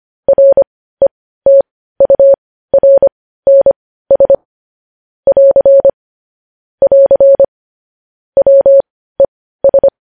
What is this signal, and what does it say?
Morse code, heard as a keyed tone, about 25 wpm, spaced as for 15 wpm, 550 Hz. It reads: RETURNS <AR> <AR> WES
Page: news-headlines-in-morse-code-at-15-wpm